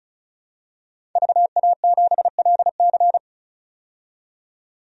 Morse code, heard as a keyed tone, about 34 wpm, 700 Hz. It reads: VA7LC
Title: DXCC-100 Call Signs - 5 Characters - 35wpm